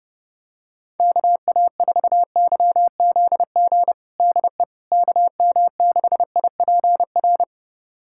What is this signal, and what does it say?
Morse code, heard as a keyed tone, about 32 wpm, 700 Hz. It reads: KA4YZG DE KM6IPR
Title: USA Callsign DE Callsign - 6 Characters - 30wpm